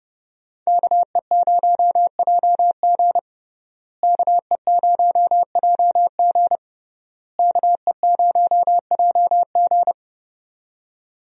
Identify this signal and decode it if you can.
Morse code, heard as a keyed tone, about 32 wpm, 700 Hz. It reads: KE0JG KE0JG KE0JG
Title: USA Call Signs - 5 Characters - Repeated 3x - 30wpm